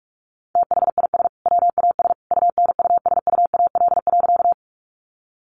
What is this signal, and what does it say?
Morse code, heard as a keyed tone, about 44 wpm, 700 Hz. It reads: THIS WAS UNUSUAL.
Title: Sentences from Top 1400 Words 45wpm